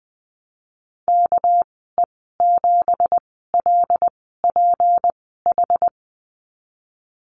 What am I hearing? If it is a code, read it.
Morse code, heard as a keyed tone, about 19 wpm, spaced as for 14 wpm, 700 Hz. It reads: KE7LPH
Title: 238 Lesson 38g - Letter q - Call Signs 20wpm